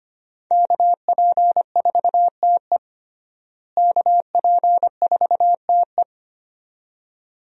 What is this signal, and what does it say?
Morse code, heard as a keyed tone, about 25 wpm, 700 Hz. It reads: KP4TE KP4TE
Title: DXCC-100 Call Signs - 5 Characters - Repeated 2x - 25wpm